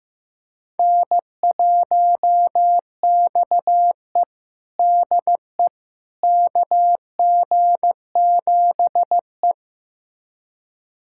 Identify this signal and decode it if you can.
Morse code, heard as a keyed tone, about 16 wpm, 700 Hz. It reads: N1XE DE KG7E
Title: USA Callsign DE Callsign - 4 Characters - 15wpm